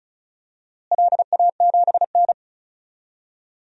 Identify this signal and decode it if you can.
Morse code, heard as a keyed tone, about 36 wpm, 700 Hz. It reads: LA7N
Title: DXCC-100 Call Signs - 4 Characters - Repeated 3x - 35wpm